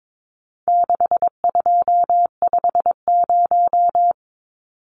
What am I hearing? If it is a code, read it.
Morse code, heard as a keyed tone, about 22 wpm, 700 Hz. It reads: 6250
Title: Numbers - 4-Digit 22wpm